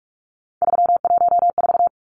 Morse code, heard as a keyed tone, about 44 wpm, 700 Hz. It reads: 314
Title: Numbers - 3-Digit 45wpm